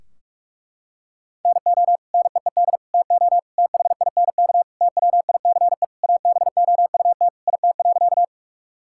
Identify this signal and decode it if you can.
Morse code, heard as a keyed tone, about 41 wpm, 700 Hz. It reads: NO NEED TO THINK TWICE ABOUT IT.
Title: Sentences from Top 1600 Words 45wpm